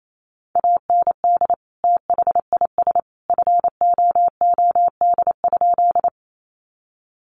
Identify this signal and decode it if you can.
Morse code, heard as a keyed tone, about 28 wpm, 700 Hz. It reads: AND THIS FOOD?
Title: Sentences from Top 300 Words 28wpm